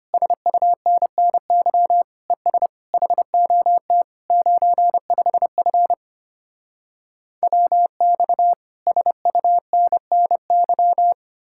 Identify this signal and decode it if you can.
Morse code, heard as a keyed tone, about 33 wpm, 700 Hz. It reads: SUNNY ES HOT 95F WX SUNNY